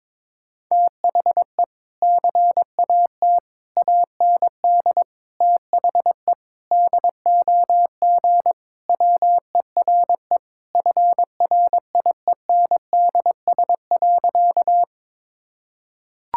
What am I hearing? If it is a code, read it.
Morse code, heard as a keyed tone, about 22 wpm, 700 Hz. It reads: THE CAT AND THE DOG WERE FRIENDS.
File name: practice / 22wpm / Sentences with 2-Word Phrases - 22wpm